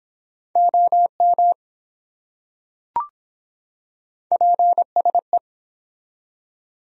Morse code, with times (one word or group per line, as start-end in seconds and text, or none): 0.6-1.5 OM
4.3-5.4 PSE